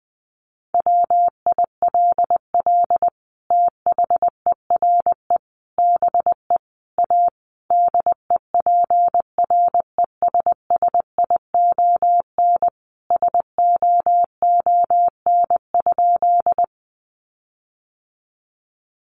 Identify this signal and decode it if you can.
Morse code, heard as a keyed tone, about 20 wpm, 700 Hz. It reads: WILL THERE BE A DEPRESSION SOON?
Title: Sentences from Top 1300 Words 20wpm